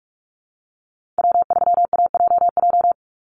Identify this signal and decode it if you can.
Morse code, heard as a keyed tone, about 44 wpm, 700 Hz. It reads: W3AJJ
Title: USA Call Signs 45wpm